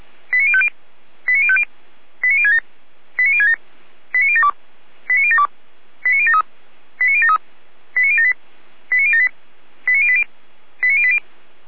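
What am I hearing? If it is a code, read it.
Five-tone selective call (ZVEI), heard as quick runs of five tones, tones of about 70 ms. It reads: 89050 89050 89067 89067 89072 89072 89073 89073 89078 89078 89080 89080
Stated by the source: ZVEI five-tone selective call 89022 received via SDR, from feuerwehr-orlamuende (Feuerwehr Orlamünde fire brigade).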